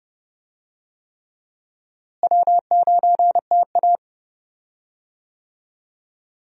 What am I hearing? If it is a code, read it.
Morse code, heard as a keyed tone, about 32 wpm, 700 Hz. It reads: W9TA
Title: USA Call Signs - 4 Characters - Repeated 3x - 30wpm